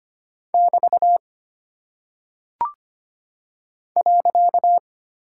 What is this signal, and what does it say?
Morse code, heard as a keyed tone, about 25 wpm, 700 Hz. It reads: <BT> .
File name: Single Punctuation-Prosign - 25wpm